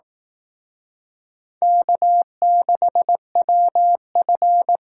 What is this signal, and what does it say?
Morse code, heard as a keyed tone, about 19 wpm, 700 Hz. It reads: K6WF
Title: Speed Racing - USA 1x2 Call Signs 15wpm